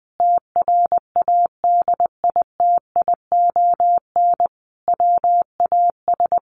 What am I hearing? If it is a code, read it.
Morse code, heard as a keyed tone, about 20 wpm, 700 Hz. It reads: TRADITION WAS
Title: Sentences from Top 1500 Words 20wpm